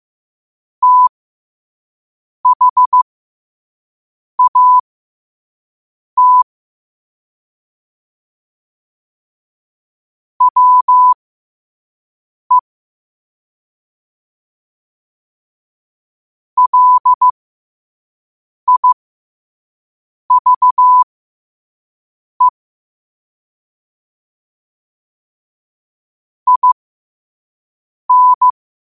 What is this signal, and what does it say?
Morse code, heard as a keyed tone, about 15 wpm, spaced as for 5 wpm, 1000 Hz. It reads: THAT WE LIVE IN
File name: qotdfiles / 5wpm